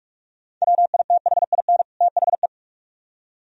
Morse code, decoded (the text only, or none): WITHIN THE